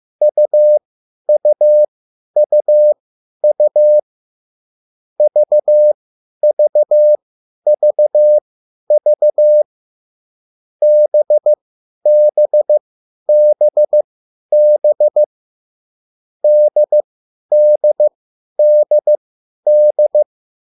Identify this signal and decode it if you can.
Morse code, heard as a keyed tone, about 15 wpm, spaced as for 10 wpm, 600 Hz. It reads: UUUU VVVV BBBB DDDD